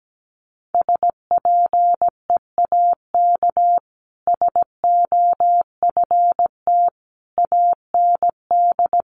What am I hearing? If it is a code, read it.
Morse code, heard as a keyed tone, about 17 wpm, 700 Hz. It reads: SPEAK SOFT AND
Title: Sentences from Top 1500 Words 17wpm